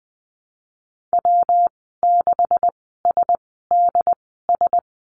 Morse code, heard as a keyed tone, about 19 wpm, spaced as for 14 wpm, 700 Hz. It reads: W6SDS
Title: 186 Lesson 30g - Letter b - Call Signs 20wpm